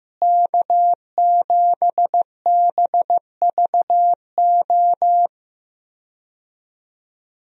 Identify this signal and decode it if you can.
Morse code, heard as a keyed tone, about 15 wpm, 700 Hz. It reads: K7BVO
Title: USA Call Signs - 5 Characters - Repeated 2x - 15wpm